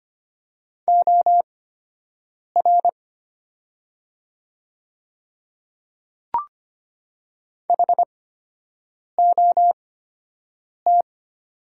Morse code, heard as a keyed tone, about 25 wpm, spaced as for 7 wpm, 700 Hz. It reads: OR HOT
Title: 8x Character Spacing - Top 300 Words - 25wpm